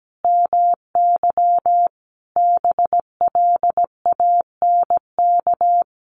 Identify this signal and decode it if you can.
Morse code, heard as a keyed tone, about 17 wpm, 700 Hz. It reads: MY BLANK
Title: Sentences from Top 1500 Words 17wpm